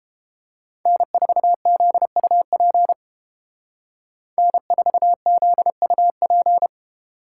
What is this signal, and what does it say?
Morse code, heard as a keyed tone, about 30 wpm, 700 Hz. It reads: N4ZUP N4ZUP